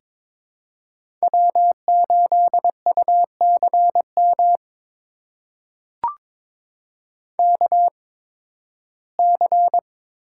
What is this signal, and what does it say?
Morse code, heard as a keyed tone, about 22 wpm, spaced as for 14 wpm, 700 Hz. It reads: W8UCM KC
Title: USA Call Signs - 8x Character Spacing - 22wpm